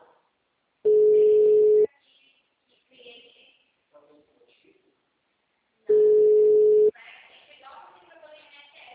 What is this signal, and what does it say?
Ringback tone (Europe, 425 Hz), with speech.